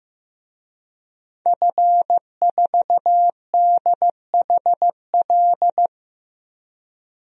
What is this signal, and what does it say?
Morse code, heard as a keyed tone, about 15 wpm, 700 Hz. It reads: F4DHL
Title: DXCC-100 Call Signs - 5 Characters - 15wpm